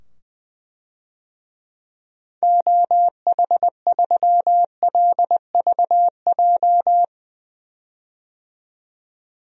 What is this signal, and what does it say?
Morse code, heard as a keyed tone, about 21 wpm, 700 Hz. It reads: OH3LVJ
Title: DXCC-100 Call Signs - 6 Characters - 20wpm